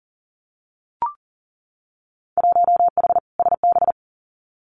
Morse code, heard as a keyed tone, about 40 wpm, 700 Hz. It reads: 1H SB
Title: Winter Field Day Exchanges 40wpm